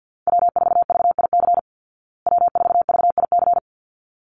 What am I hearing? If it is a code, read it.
Morse code, heard as a keyed tone, about 50 wpm, 700 Hz. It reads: W4VIC W4VIC
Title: Call Signs Repeated 3x - 50wpm